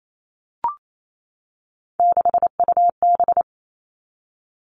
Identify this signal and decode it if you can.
Morse code, heard as a keyed tone, about 28 wpm, 700 Hz. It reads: 6UB
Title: All Cut Numbers - 3-Digit 28wpm